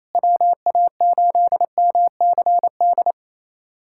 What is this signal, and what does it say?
Morse code, heard as a keyed tone, about 28 wpm, 700 Hz. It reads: WA8MCD
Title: USA Callsign DE Callsign 28wpm